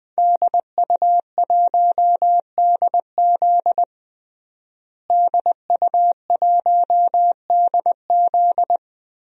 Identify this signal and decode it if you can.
Morse code, heard as a keyed tone, about 20 wpm, 700 Hz. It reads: DU1DZ DU1DZ
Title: DXCC-100 Call Signs - 5 Characters - Repeated 3x - 20wpm